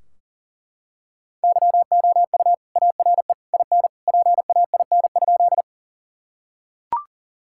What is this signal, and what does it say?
Morse code, heard as a keyed tone, about 40 wpm, 700 Hz. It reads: YOU ARE IN PAIN?
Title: Sentences from Top 1600 Words 40wpm